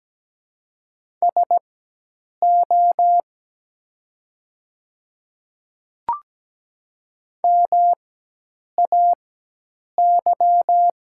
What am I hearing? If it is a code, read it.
Morse code, heard as a keyed tone, about 17 wpm, spaced as for 8 wpm, 700 Hz. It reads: SO MAY
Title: 4x Character Spacing - Top 100 Words - 17wpm